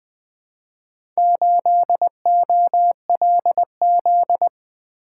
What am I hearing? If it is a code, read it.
Morse code, heard as a keyed tone, about 21 wpm, 700 Hz. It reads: 8OLZ
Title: Four Letter-Number Combinations - 20wpm